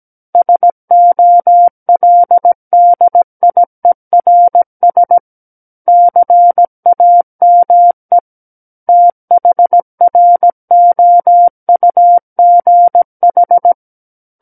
Morse code, heard as a keyed tone, about 17 wpm, 700 Hz. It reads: SOLDIERS CAME THROUGH
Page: SKCC Learning Center - Morse Code Practice Files